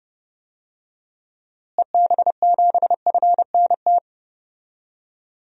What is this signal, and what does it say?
Morse code, heard as a keyed tone, about 30 wpm, 700 Hz. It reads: EB7FNT
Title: DXCC-100 Call Signs - 6 Characters - Repeated 2x - 30wpm